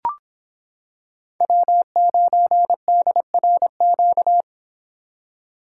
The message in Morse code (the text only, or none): W9DRQ